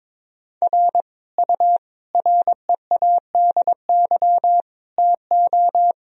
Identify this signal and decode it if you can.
Morse code, heard as a keyed tone, about 22 wpm, 700 Hz. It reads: R U READY TO
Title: Sentences from Top 400 Words and Abbreviations 22wpm